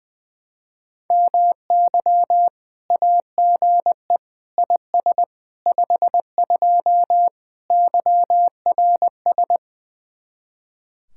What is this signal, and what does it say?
Morse code, heard as a keyed tone, about 20 wpm, 700 Hz. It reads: MY AGE IS 52 YRS
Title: Speed-Racing Common QSO Phrases - 20wpm